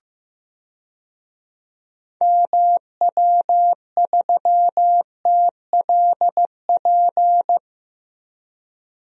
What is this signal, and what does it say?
Morse code, heard as a keyed tone, about 15 wpm, 700 Hz. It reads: MW3TLP
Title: DXCC-100 Call Signs - 6 Characters - Repeated 2x - 15wpm